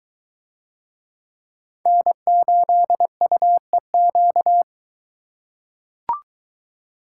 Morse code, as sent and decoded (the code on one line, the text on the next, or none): -. ---.. ..- . --.-
N8UEQ